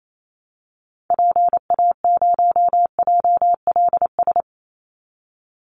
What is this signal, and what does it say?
Morse code, heard as a keyed tone, about 28 wpm, 700 Hz. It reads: PA0JLS
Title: Call Signs Repeated 3x - 28wpm